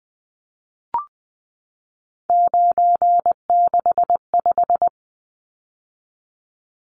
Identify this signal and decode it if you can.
Morse code, heard as a keyed tone, about 20 wpm, 700 Hz. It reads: 965
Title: Numbers - 3-Digit 20wpm